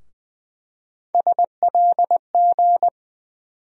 Morse code, heard as a keyed tone, about 20 wpm, 700 Hz. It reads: SLG
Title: USA Call Sign Postfixes - 3 Letters 20wpm